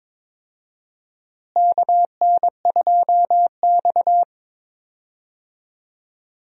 Morse code, heard as a keyed tone, about 22 wpm, 700 Hz. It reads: KN2X
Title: USA Call Signs - 4 Characters - Repeated 3x - 22wpm